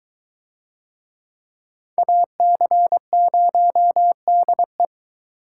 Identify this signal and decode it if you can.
Morse code, heard as a keyed tone, about 20 wpm, 700 Hz. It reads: AC0DE